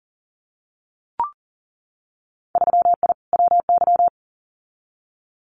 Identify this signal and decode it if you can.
Morse code, heard as a keyed tone, about 40 wpm, 700 Hz. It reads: 3I WY
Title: Winter Field Day Exchanges Encore 40wpm